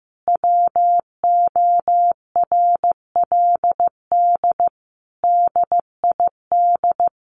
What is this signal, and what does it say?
Morse code, heard as a keyed tone, about 15 wpm, 700 Hz. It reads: WORLD DID
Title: Sentences from Top 200 Words 15wpm